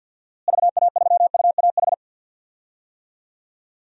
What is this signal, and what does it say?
Morse code, heard as a keyed tone, about 52 wpm, 700 Hz. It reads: VA3UAH
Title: DXCC-100 Call Signs - 6 Characters - Repeated 3x - 50wpm